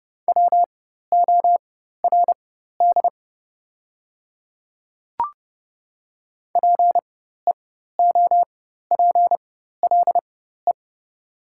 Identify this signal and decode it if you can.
Morse code, heard as a keyed tone, about 30 wpm, spaced as for 14 wpm, 700 Hz. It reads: WORD PEOPLE
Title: 4x Character Spacing - Top 300 Words - 30wpm